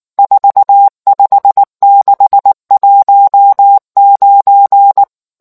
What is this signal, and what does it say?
Morse code, heard as a keyed tone, about 19 wpm, 800 Hz. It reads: 45619